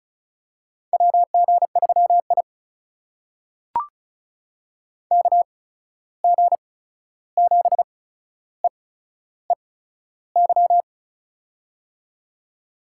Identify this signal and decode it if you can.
Morse code, heard as a keyed tone, about 36 wpm, spaced as for 13 wpm, 700 Hz. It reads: WG3I KG7EEY
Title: USA Call Signs - 8x Character Spacing - 35wpm